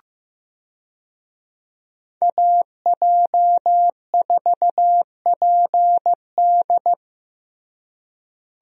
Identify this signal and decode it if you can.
Morse code, heard as a keyed tone, about 15 wpm, 700 Hz. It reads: AJ4PD
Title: USA Call Signs - 5 Characters - Repeated 2x - 15wpm